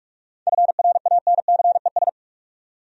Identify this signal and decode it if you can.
Morse code, heard as a keyed tone, about 42 wpm, 700 Hz. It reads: FRANCES
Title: USA Common Names - 45wpm